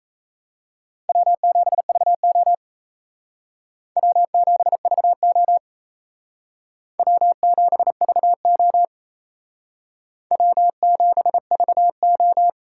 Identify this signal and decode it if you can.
Morse code, heard as a keyed tone, about 36 wpm, 700 Hz. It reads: W7VO W7VO W7VO W7VO